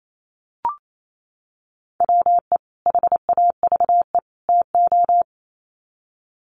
Morse code, heard as a keyed tone, about 28 wpm, 700 Hz. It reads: WE HAVE TO
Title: Sets of 3 Words 28wpm